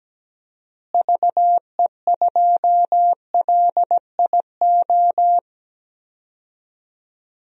Morse code, heard as a keyed tone, about 17 wpm, 700 Hz. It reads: VE2LIO
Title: DXCC-100 Call Signs - 6 Characters - Repeated 2x - 17wpm